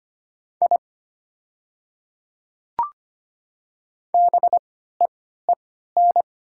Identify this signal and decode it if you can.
Morse code, heard as a keyed tone, about 25 wpm, spaced as for 14 wpm, 700 Hz. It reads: I BEEN
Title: 3x Character Spacing - Top 100 Words - 25wpm